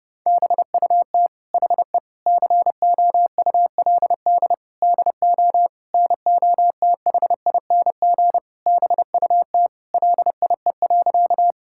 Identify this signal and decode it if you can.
Morse code, heard as a keyed tone, about 30 wpm, 700 Hz. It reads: BUT HE COULD DO NOTHING BUT LIE.
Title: Sentences from Top 1600 Words 30wpm